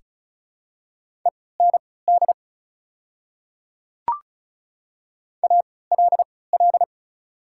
Morse code, heard as a keyed tone, about 36 wpm, spaced as for 20 wpm, 700 Hz. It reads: END ALL
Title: 3x Character Spacing - Top 200 Words - 35wpm